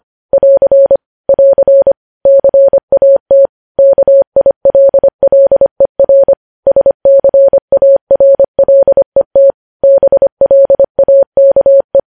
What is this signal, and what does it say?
Morse code, heard as a keyed tone, about 25 wpm, 550 Hz. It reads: <AR> <AR> CAT KILLER SCARLET BLAKE
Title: News Headlines at 25